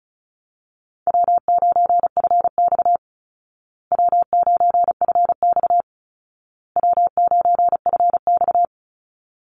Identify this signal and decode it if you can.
Morse code, heard as a keyed tone, about 34 wpm, 700 Hz. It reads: W9FX W9FX W9FX
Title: Easy Call Signs Repeated 3x - 35wpm